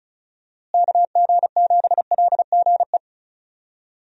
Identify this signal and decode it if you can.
Morse code, heard as a keyed tone, about 34 wpm, 700 Hz. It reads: KG7LGE
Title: USA Call Signs - 6 Characters - Repeated 3x - 35wpm